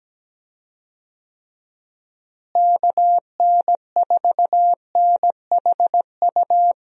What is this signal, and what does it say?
Morse code, heard as a keyed tone, about 17 wpm, 700 Hz. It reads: KN4NHU
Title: USA Call Signs - 6 Characters - Repeated 2x - 17wpm